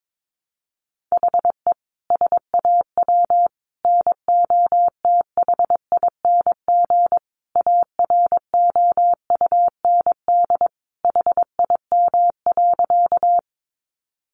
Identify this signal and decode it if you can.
Morse code, heard as a keyed tone, about 22 wpm, 700 Hz. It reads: HE SAW NOTHING AROUND HIM.
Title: Sentences from Top 600 Words 22wpm